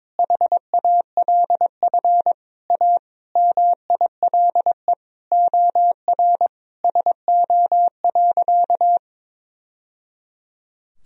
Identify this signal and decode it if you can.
Morse code, heard as a keyed tone, about 21 wpm, 700 Hz. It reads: HALF A MILE OR SO.
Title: Sentences from Top 1000 Words - Review Part 1 - 22wpm